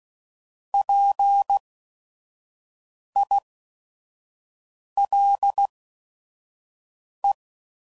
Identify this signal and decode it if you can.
Morse code, heard as a keyed tone, about 16 wpm, spaced as for 5 wpm, 800 Hz. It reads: PILE
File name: Prefixes / By First Letter / Ps